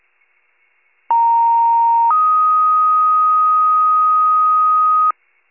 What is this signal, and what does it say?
Two-tone page, heard as a one-second tone, then a three-second tone, about 930 then 1250 Hz. It is heard from 1.1 to 5.1 s.
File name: pager / 39 RESCUE / wcalert